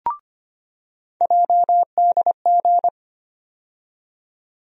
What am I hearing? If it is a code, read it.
Morse code, heard as a keyed tone, about 25 wpm, 700 Hz. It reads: JDG
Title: USA Call Sign Postfixes - 3 Letters 25wpm